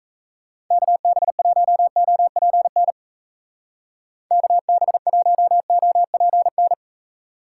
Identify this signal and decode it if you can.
Morse code, heard as a keyed tone, about 37 wpm, 700 Hz. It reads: KB1OPN KB1OPN